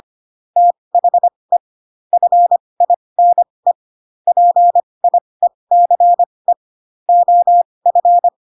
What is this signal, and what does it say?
Morse code, heard as a keyed tone, about 25 wpm, spaced as for 20 wpm, 700 Hz. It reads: THE FINE PIECE OF
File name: Pinocchio / Pinocchio ch2 15 25 ik1hge